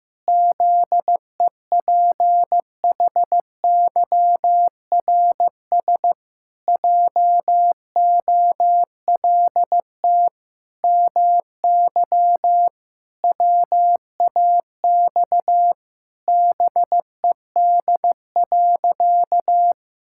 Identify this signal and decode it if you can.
Morse code, heard as a keyed tone, about 16 wpm, 700 Hz. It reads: ZEPHYRS JOLT MY WAX BED.
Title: Pangram Challenge v2 - 15wpm